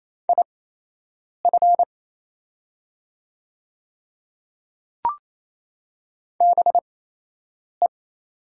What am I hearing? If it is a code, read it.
Morse code, heard as a keyed tone, about 28 wpm, spaced as for 8 wpm, 700 Hz. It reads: IF BE
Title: 8x Character Spacing - Top 100 Words - 28wpm